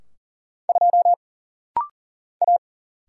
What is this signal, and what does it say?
Morse code, heard as a keyed tone, about 39 wpm, 700 Hz. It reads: 2 A
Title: Single Letter-Number Alt - Rapid-Fire 40wpm